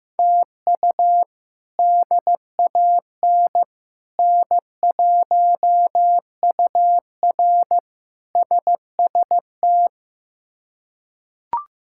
Morse code, heard as a keyed tone, about 15 wpm, 700 Hz. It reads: TU DAN N1UR SST
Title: K1USN SST QSO Practice - 15wpm